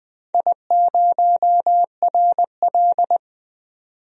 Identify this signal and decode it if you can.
Morse code, heard as a keyed tone, about 21 wpm, 700 Hz. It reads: I0RL
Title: DXCC-100 Call Signs - 4 Characters - 20wpm